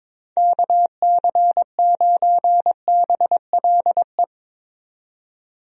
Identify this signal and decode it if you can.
Morse code, heard as a keyed tone, about 22 wpm, 700 Hz. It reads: KC9BLE
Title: USA Call Signs - 6 Characters - 22wpm